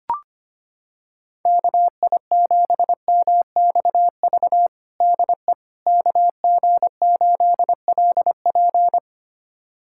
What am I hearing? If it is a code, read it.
Morse code, heard as a keyed tone, about 25 wpm, 700 Hz. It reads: KI7MXV DE KG8LP
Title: USA Callsign DE Callsign 25wpm